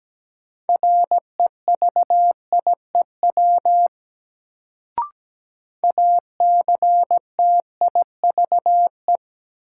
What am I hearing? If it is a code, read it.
Morse code, heard as a keyed tone, about 17 wpm, 700 Hz. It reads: REVIEW ACTIVE
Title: Sentences from Top 900 Words - New Word Drill - 17wpm